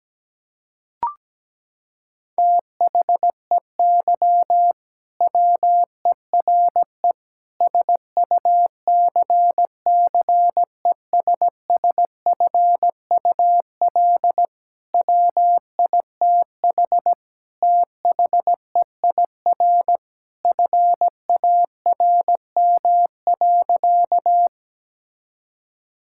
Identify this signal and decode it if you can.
Morse code, heard as a keyed tone, about 17 wpm, 700 Hz. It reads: THEY WERE SUCCESSFUL WITH THEIR FARM.
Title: Sentences from Top 1000 Words - Review Part 6 - 17wpm